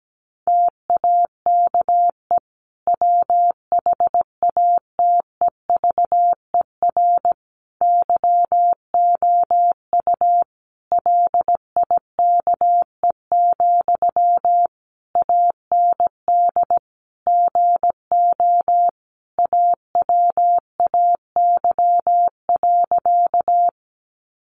Morse code, heard as a keyed tone, about 17 wpm, 700 Hz. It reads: TAKE WHATEVER YOU LIKE, AND GO AWAY.
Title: Sentences from Top 1100 Words 17wpm